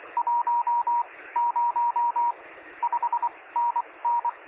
Morse code, heard as a keyed tone, about 23 wpm, spaced as for 18 wpm, 950 Hz. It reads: 1 0 5NN
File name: sm7lcb / 0423z yl2aj 23cm